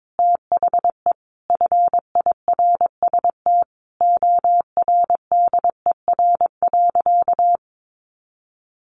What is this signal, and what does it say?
Morse code, heard as a keyed tone, about 22 wpm, 700 Hz. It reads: THE FIRST ORDER.
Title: Sentences from Top 1100 Words 22wpm